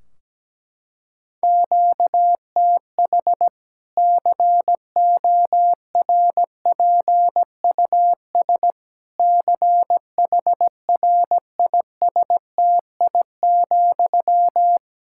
Morse code, heard as a keyed tone, about 17 wpm, 700 Hz. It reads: QTH CORPUS CHRISTI,
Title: Practice QSO - Exchange 1 - Beginning - 17wpm